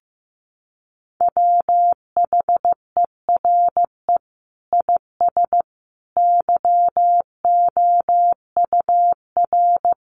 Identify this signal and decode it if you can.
Morse code, heard as a keyed tone, about 15 wpm, 700 Hz. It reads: WHERE IS YOUR
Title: Sentences from Top 1200 Words 15wpm